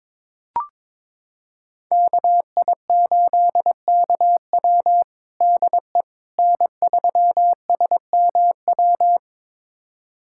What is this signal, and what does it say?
Morse code, heard as a keyed tone, about 22 wpm, 700 Hz. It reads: KI8KW DE N3SMW
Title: USA Callsign DE Callsign - 5 Characters - 22wpm